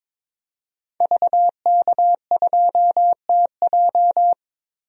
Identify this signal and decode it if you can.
Morse code, heard as a keyed tone, about 22 wpm, 700 Hz. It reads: VK2TJ
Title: DXCC-100 Call Signs - 5 Characters - Repeated 2x - 22wpm